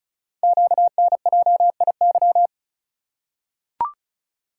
Morse code, heard as a keyed tone, about 35 wpm, 690 Hz. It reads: QNJIY
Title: Code Groups - 5 Letters - 35wpm